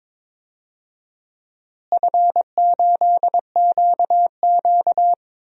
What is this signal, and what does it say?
Morse code, heard as a keyed tone, about 22 wpm, 700 Hz. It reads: F8QQ
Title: DXCC-100 Call Signs - 4 Characters - Repeated 2x - 22wpm